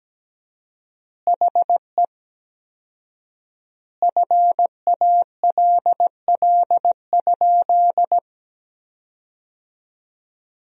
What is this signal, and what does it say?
Morse code, heard as a keyed tone, about 17 wpm, 700 Hz. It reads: HE FALL?
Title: 4x Word-Spacing - Sentences from Top 500 Words - 17wpm